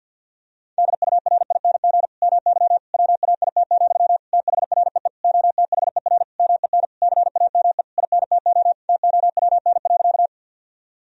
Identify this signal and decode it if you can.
Morse code, heard as a keyed tone, about 48 wpm, 700 Hz. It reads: DURING MY WAIT, THREE OTHER MEN CAME INTO TOWN.
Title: Sentences from Top 1000 Words - Review Part 2 - 50wpm